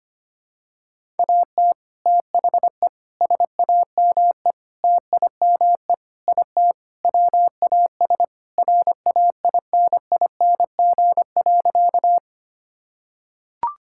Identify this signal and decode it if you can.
Morse code, heard as a keyed tone, about 25 wpm, 700 Hz. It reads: AT THE SAME TIME IT WAS RAINING.
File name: practice / 25wpm / Sentences with 3-Word Phrases - 25wpm